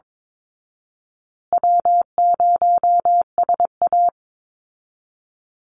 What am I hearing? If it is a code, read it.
Morse code, heard as a keyed tone, about 22 wpm, 700 Hz. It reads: W0SA
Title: Easy Call Signs Repeated 2x - 22wpm